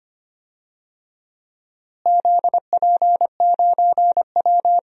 Morse code, heard as a keyed tone, about 25 wpm, 700 Hz. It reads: ZP9W
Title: DXCC-100 Call Signs - 4 Characters - Repeated 3x - 25wpm